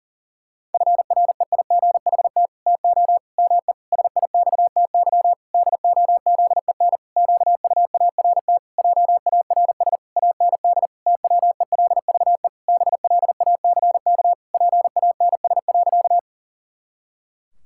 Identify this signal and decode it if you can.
Morse code, heard as a keyed tone, about 40 wpm, 700 Hz. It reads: FREIGHT TO ME SIXTY DOZEN QUART JARS AND TWELVE BLACK PANS.
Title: Pangram Challenge v2 - 40wpm